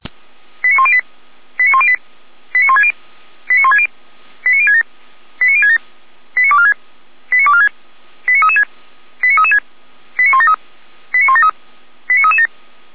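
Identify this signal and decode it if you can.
Five-tone selective call (ZVEI), heard as quick runs of five tones, tones of about 70 ms. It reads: 89108 89108 89160 89160 89067 89067 89246 89246 89336 89336 89173 89173 89337
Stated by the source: ZVEI five-tone selective call received on an SDR.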